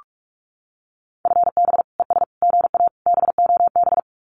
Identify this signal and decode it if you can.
Morse code, heard as a keyed tone, about 46 wpm, 700 Hz. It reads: FB ES GA BOB
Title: Common QSO Phrases Encore 45wpm